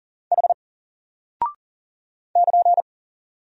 Morse code, heard as a keyed tone, about 40 wpm, 700 Hz. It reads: H <KN>
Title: Single Character (Letter:Number:Punctuation:Prosign) Alt 40wpm